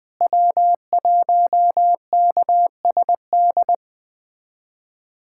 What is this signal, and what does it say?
Morse code, heard as a keyed tone, about 20 wpm, 700 Hz. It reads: W1KSD
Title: USA Callsign DE Callsign 20wpm